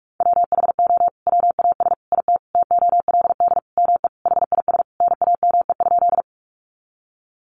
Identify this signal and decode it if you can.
Morse code, heard as a keyed tone, about 44 wpm, 700 Hz. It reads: WHO WAS IT TOLD ME HIS NAME?
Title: Sentences from Top 400 Words 45wpm